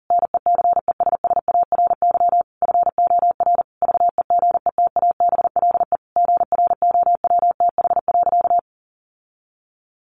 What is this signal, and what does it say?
Morse code, heard as a keyed tone, about 40 wpm, 700 Hz. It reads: NECESSARY FOR VEGETABLE GROWTH.
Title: Sentences from Top 1300 Words 40wpm